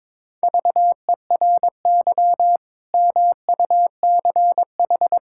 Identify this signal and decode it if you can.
Morse code, heard as a keyed tone, about 22 wpm, 700 Hz. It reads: VERY MUCH
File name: practice / 22wpm / Build A Phrase - 5-Word Phrases - 22wpm